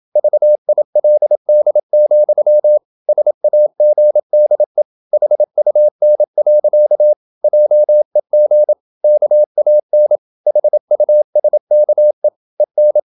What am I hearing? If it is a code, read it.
Morse code, heard as a keyed tone, about 27 wpm, 600 Hz. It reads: VILD, SAGDE HUN. JEG KAN HUSKE EN
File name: Ferie QRQ 27wpm